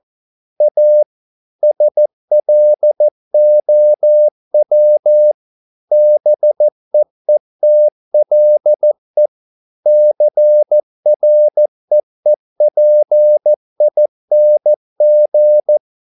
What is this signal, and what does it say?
Morse code, heard as a keyed tone, about 14 wpm, 600 Hz. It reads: A SLOW BEETLE CREEPING